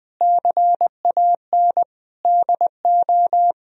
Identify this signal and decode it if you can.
Morse code, heard as a keyed tone, about 20 wpm, 700 Hz. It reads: CAN DO
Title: Sentences from Top 1000 Words - Review Part 3 - 20wpm